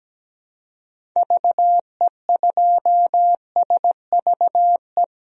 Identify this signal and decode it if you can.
Morse code, heard as a keyed tone, about 17 wpm, 700 Hz. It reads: VE2SVE
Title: DXCC-100 Call Signs - 6 Characters - 17wpm